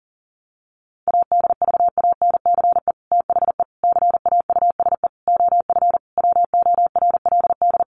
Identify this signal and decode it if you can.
Morse code, heard as a keyed tone, about 40 wpm, 700 Hz. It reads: ADVANCE THE CAUSE OF WORLD
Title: Sentences from Top 1400 Words 40wpm